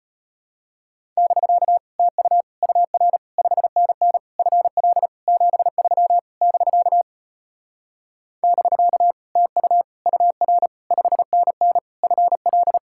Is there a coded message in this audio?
Yes: Morse code from 1.2 to 12.8 s, heard as a keyed tone, about 33 wpm, 700 Hz. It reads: <BK> TU UR 5NN FL 73 <BK> <BK> TU UR 5NN FL